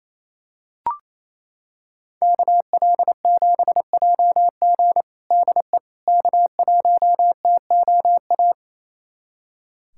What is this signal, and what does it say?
Morse code, heard as a keyed tone, about 29 wpm, 700 Hz. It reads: KL7JG DE K1TOA
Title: USA Callsign DE Callsign - 5 Characters - 28wpm